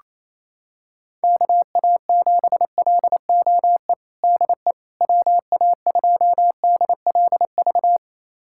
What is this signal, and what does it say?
Morse code, heard as a keyed tone, about 28 wpm, 700 Hz. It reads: KA7LOE DE WA2DLV
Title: USA Callsign DE Callsign - 6 Characters - 28wpm